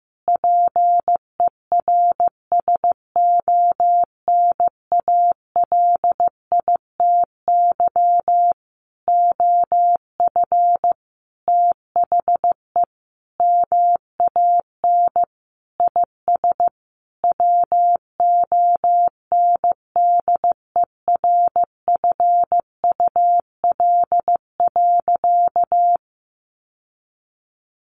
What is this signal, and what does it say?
Morse code, heard as a keyed tone, about 15 wpm, 700 Hz. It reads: PERSONALITY OF THE MAN IS WONDERFUL.
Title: Sentences from Top 1400 Words 15wpm